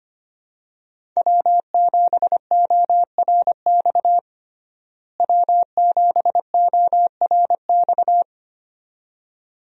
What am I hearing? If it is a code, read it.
Morse code, heard as a keyed tone, about 25 wpm, 700 Hz. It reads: W7ORX W7ORX
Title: USA Call Signs - 5 Characters - Repeated 2x - 25wpm